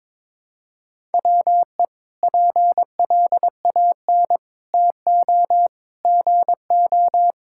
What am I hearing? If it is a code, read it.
Morse code, heard as a keyed tone, about 21 wpm, 700 Hz. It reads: WE PLAN TO GO
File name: practice / 22wpm / Sentences with 4-Word Phrases - 22wpm